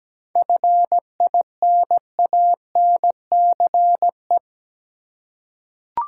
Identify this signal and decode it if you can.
Morse code, heard as a keyed tone, about 17 wpm, 700 Hz. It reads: FINANCE
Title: Seven Letter Words - 17wpm